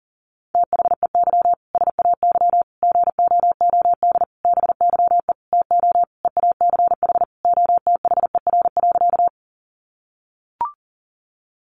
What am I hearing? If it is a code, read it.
Morse code, heard as a keyed tone, about 37 wpm, 700 Hz. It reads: THEY SAY GOOD BYE TO EACH OTHER.
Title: Easy Phrases 40wpm